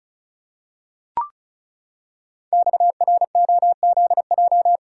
Morse code, heard as a keyed tone, about 36 wpm, 700 Hz. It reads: XROZJ
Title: Code Groups - 5 Letters - 35wpm